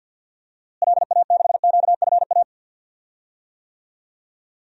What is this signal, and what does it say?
Morse code, heard as a keyed tone, about 46 wpm, 700 Hz. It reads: LA6QFA
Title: DXCC-100 Call Signs - 6 Characters - Repeated 2x - 50wpm